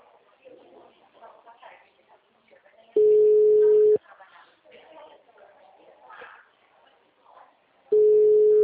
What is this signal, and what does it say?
Ringback tone (Europe, 425 Hz), with speech.
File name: out- -2011- - - .357